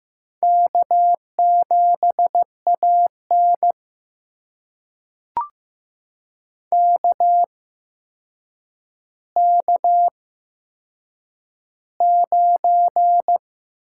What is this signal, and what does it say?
Morse code, heard as a keyed tone, about 15 wpm, spaced as for 7 wpm, 700 Hz. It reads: K7AN KK9
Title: USA Call Signs - 4 Characters - 8x Character Spacing - 15wpm